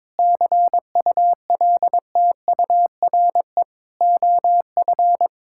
Morse code, heard as a keyed tone, about 22 wpm, 700 Hz. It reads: CULTURE OF
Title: Sentences from Top 1000 Words - Review Part 6 - 22wpm